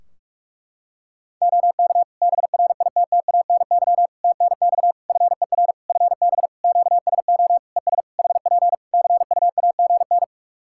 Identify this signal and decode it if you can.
Morse code, heard as a keyed tone, about 46 wpm, 700 Hz. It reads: OK BRITTANY TNX FER FB QSO ES HP CUAGN
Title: K1USN SST QSO Practice 45wpm